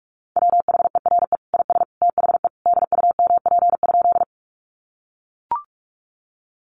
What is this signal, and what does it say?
Morse code, heard as a keyed tone, about 44 wpm, 700 Hz. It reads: WHERE IS THE DUMP?
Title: Sentences from Top 1300 Words 45wpm